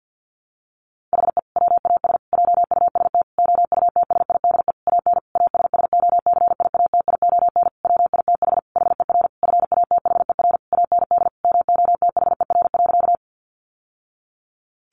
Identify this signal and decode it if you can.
Morse code, heard as a keyed tone, about 50 wpm, 700 Hz. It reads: HE WAS JUST OUTSIDE AN ASSOCIATION WITH HER FATHER AND MOTHER.
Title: Sentences from Top 1100 Words 50wpm